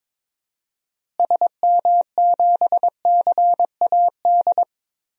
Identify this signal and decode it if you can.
Morse code, heard as a keyed tone, about 22 wpm, 700 Hz. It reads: SM7CAD
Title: DXCC-100 Call Signs - 6 Characters - Repeated 2x - 22wpm